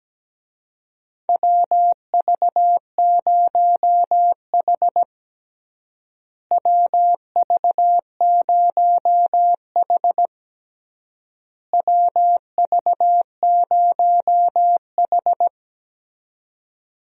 Morse code, heard as a keyed tone, about 17 wpm, 700 Hz. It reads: WV0H WV0H WV0H
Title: USA Call Signs - 4 Characters - Repeated 3x - 17wpm